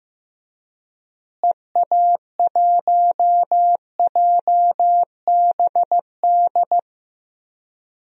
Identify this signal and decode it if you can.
Morse code, heard as a keyed tone, about 15 wpm, 700 Hz. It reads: EA1JBD
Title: DXCC-100 Call Signs - 6 Characters - Repeated 2x - 15wpm